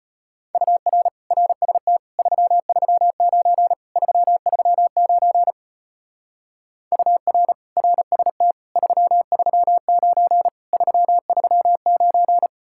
Morse code, heard as a keyed tone, about 32 wpm, 700 Hz. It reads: UR RST 339 339 UR RST 339 339